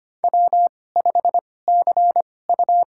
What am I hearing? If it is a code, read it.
Morse code, heard as a keyed tone, about 25 wpm, spaced as for 18 wpm, 700 Hz. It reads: W5CU
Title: USA Call Signs - 4 Characters - 2x Character Spacing - 25wpm